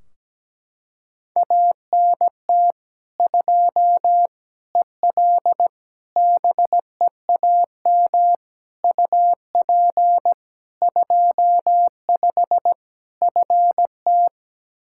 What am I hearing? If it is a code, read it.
Morse code, heard as a keyed tone, about 17 wpm, 700 Hz. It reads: ANT 2 EL BEAM UP 25 FT
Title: Practice QSO - Exchange 2 - Info - 17wpm